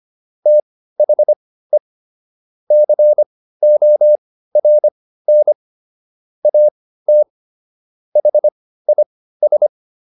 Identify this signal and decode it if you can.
Morse code, heard as a keyed tone, about 25 wpm, spaced as for 15 wpm, 600 Hz. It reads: THE CORN AT HIS